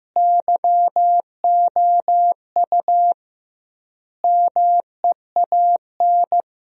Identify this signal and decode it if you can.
Morse code, heard as a keyed tone, about 16 wpm, 700 Hz. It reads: YOU MEAN
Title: 2x Word-Spacing - Sentences from Top 400 Words - 15wpm